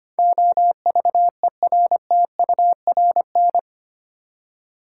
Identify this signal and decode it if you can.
Morse code, heard as a keyed tone, about 25 wpm, 700 Hz. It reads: OVERTURN
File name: Eight Letter Words - 25wpm